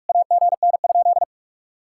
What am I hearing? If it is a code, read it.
Morse code, heard as a keyed tone, about 60 wpm, spaced as for 53 wpm, 700 Hz. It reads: AGN?
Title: Speed-Racing Common QSO Phrases Encore - 45wpm